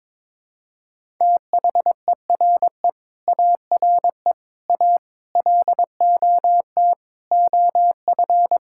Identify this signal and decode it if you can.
Morse code, heard as a keyed tone, about 21 wpm, 700 Hz. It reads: THERE ARE A LOT OF
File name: practice / 22wpm / Build A Phrase - 5-Word Phrases - 22wpm